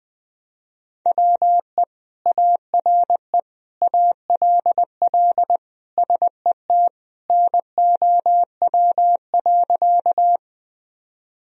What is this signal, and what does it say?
Morse code, heard as a keyed tone, about 21 wpm, 700 Hz. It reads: WE ARE ALL SET NOW.
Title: Sentences from Top 1000 Words - Review Part 1 - 20wpm